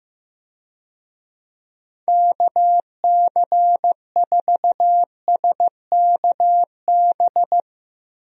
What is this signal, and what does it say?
Morse code, heard as a keyed tone, about 15 wpm, 700 Hz. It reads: KC4SKB
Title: USA Call Signs - 6 Characters - Repeated 2x - 15wpm